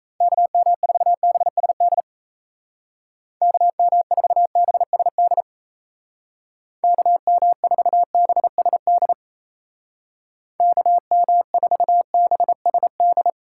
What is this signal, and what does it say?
Morse code, heard as a keyed tone, about 35 wpm, 700 Hz. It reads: KM4BSD KM4BSD KM4BSD KM4BSD
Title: Speed Racing - USA 2x3 Call Signs 28wpm